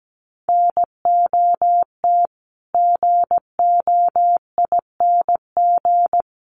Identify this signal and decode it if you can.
Morse code, heard as a keyed tone, about 17 wpm, 700 Hz. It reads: NOT GOING
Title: Sets of 5 Words Encore 17wpm